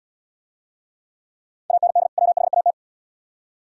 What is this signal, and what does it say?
Morse code, heard as a keyed tone, about 73 wpm, 700 Hz. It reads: AND WHAT